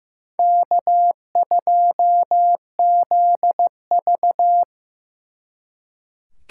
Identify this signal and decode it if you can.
Morse code, heard as a keyed tone, about 18 wpm, 700 Hz. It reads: K2ZV
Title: Speed Racing - USA 1x2 Call Signs 15wpm